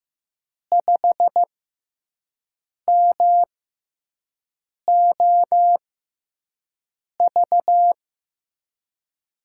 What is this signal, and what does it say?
Morse code, heard as a keyed tone, about 15 wpm, spaced as for 5 wpm, 700 Hz. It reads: 5MOV
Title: USA Call Signs - 6 Characters - 6x Character Spacing - 15wpm